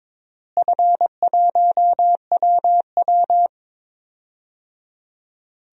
Morse code, heard as a keyed tone, about 21 wpm, 700 Hz. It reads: F1WW
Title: DXCC-100 Call Signs - 4 Characters - Repeated 3x - 22wpm